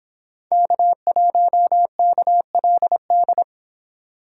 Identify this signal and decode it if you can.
Morse code, heard as a keyed tone, about 26 wpm, 700 Hz. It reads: K1KLD